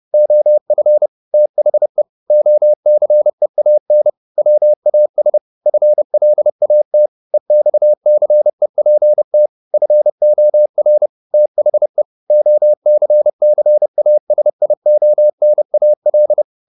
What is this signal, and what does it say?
Morse code, heard as a keyed tone, about 30 wpm, 600 Hz. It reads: OF THE OCEAN WAS FLAT EXCEPT FOR THE OCCASIONAL
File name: the old man and the sea / the old man and the sea 1 30wpm